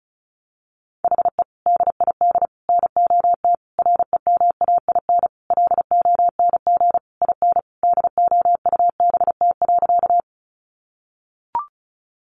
Morse code, heard as a keyed tone, about 34 wpm, 700 Hz. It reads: HE DID NOT REMAIN LONG IN DOUBT.
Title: Sentences from Top 1500 Words 35wpm